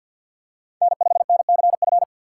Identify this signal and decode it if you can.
Morse code, heard as a keyed tone, about 50 wpm, 700 Hz. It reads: N5NCF
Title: USA Call Signs - 5 Characters - 3x Character Spacing - 50wpm